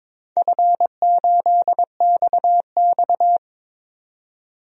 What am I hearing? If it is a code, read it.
Morse code, heard as a keyed tone, about 22 wpm, 700 Hz. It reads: F8XX
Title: DXCC-100 Call Signs - 4 Characters - 22wpm